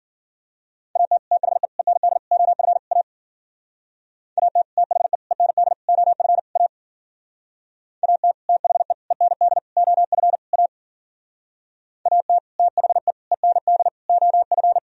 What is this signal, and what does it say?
Morse code, heard as a keyed tone, about 52 wpm, 700 Hz. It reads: AT THE END OF A AT THE END OF A AT THE END OF A AT THE END OF